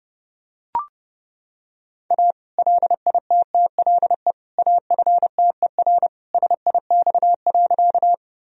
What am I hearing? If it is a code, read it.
Morse code, heard as a keyed tone, about 32 wpm, 700 Hz. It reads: A LITTLE AFTER SIX.
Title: Sentences from Top 1000 Words - Review Part 2 - 30wpm